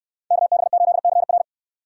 Morse code, heard as a keyed tone, about 62 wpm, 700 Hz. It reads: KB8CR